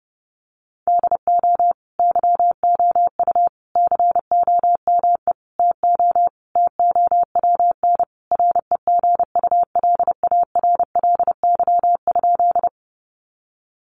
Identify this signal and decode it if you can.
Morse code, heard as a keyed tone, about 30 wpm, 700 Hz. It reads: DO YOU COME TO TOWN REGULARLY?
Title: Sentences from Top 1400 Words 30wpm